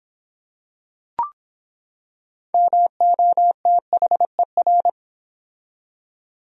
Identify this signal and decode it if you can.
Morse code, heard as a keyed tone, about 24 wpm, 700 Hz. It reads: MOTHER